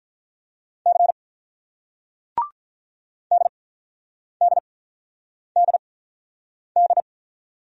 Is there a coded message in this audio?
Yes: Morse code from 0.9 to 7.0 s, heard as a keyed tone, about 44 wpm, spaced as for 9 wpm, 700 Hz. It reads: C DDDD